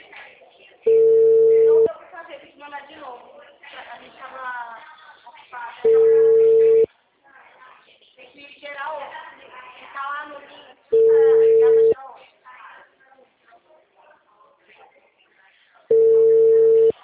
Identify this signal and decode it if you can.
Ringback tone (China, 450 Hz), with speech.